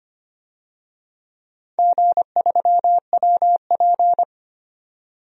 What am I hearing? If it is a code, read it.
Morse code, heard as a keyed tone, about 25 wpm, 700 Hz. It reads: G3WP
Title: DXCC-100 Call Signs - 4 Characters - 25wpm